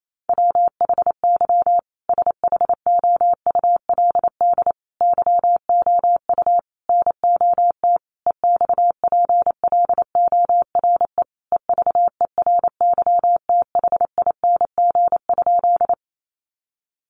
Morse code, heard as a keyed tone, about 28 wpm, 700 Hz. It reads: WHY SHOULD YOU NOT EXPLORE EVERYTHING?
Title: Sentences from Top 1100 Words 28wpm